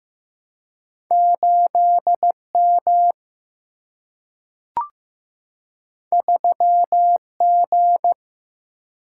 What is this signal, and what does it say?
Morse code, heard as a keyed tone, about 15 wpm, 700 Hz. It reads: 8M 3G
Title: Single-Letter Single-Number 15wpm